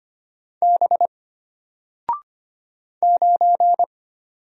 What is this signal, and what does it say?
Morse code, heard as a keyed tone, about 25 wpm, 700 Hz. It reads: B 9
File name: Single-Letter-Number-25wpm